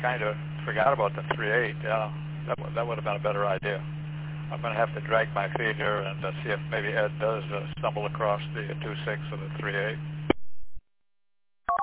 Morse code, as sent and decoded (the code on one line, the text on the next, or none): none